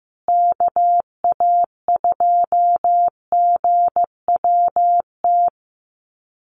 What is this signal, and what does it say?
Morse code, heard as a keyed tone, about 15 wpm, 700 Hz. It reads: KA2GWT
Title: USA Call Signs Repeated 2x 15wpm